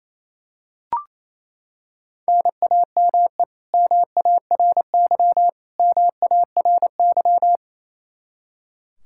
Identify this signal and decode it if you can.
Morse code, heard as a keyed tone, about 28 wpm, 700 Hz. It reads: NAME MARY MARY
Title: K1USN SST QSO Practice 28wpm